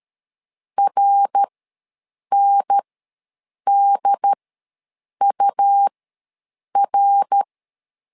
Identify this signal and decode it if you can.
Morse code, heard as a keyed tone, about 13 wpm, spaced as for 7 wpm, 800 Hz. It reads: RNDUR
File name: LEZIONI / 3 MWRO / 3 MWR ws 12wpm 1809